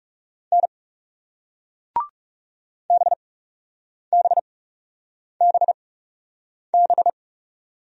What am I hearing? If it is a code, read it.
Morse code, heard as a keyed tone, about 36 wpm, spaced as for 8 wpm, 700 Hz. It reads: N BBBB
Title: Speed-Racing Single Letters 30wpm